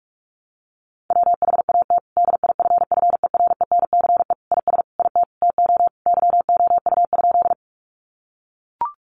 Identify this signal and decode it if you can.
Morse code, heard as a keyed tone, about 44 wpm, 700 Hz. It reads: WHAT DIFFERENCE IS IT TO YOU?
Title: Sentences from Top 800 Words 45wpm